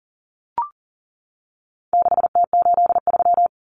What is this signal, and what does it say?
Morse code, heard as a keyed tone, about 40 wpm, 700 Hz. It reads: 6T83
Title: Common Cut Numbers - 4-Digit 40wpm